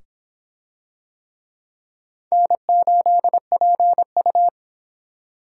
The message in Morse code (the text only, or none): N8PU